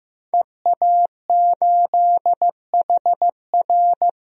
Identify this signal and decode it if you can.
Morse code, heard as a keyed tone, about 15 wpm, 700 Hz. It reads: EA8HR
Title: DXCC-100 Call Signs - 5 Characters - 15wpm